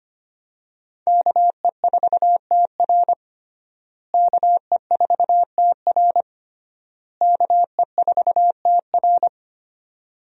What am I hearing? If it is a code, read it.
Morse code, heard as a keyed tone, about 25 wpm, 700 Hz. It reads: KE4TR KE4TR KE4TR
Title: USA Call Signs - 5 Characters - Repeated 3x - 25wpm